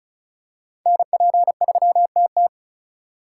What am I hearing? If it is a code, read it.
Morse code, heard as a keyed tone, about 34 wpm, 700 Hz. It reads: NP3TT
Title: USA Call Signs - 5 Characters - 3x Character Spacing - 35wpm